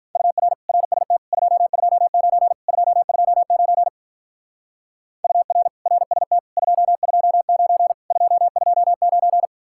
Morse code, heard as a keyed tone, about 43 wpm, 700 Hz. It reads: UR RST 229 229 UR RST 229 229
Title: Speed-Racing Common QSO Phrases - 35wpm